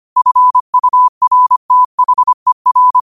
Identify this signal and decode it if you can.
Morse code, heard as a keyed tone, about 25 wpm, 1000 Hz. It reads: FURTHER
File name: qotdfiles / 25wpm